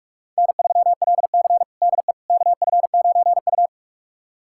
Morse code, heard as a keyed tone, about 42 wpm, 700 Hz. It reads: N3LC DE KF9U
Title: USA Callsign DE Callsign - 4 Characters - 45wpm